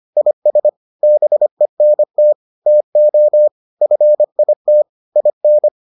Morse code, heard as a keyed tone, about 25 wpm, 600 Hz. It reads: IS BENT TO FIT IN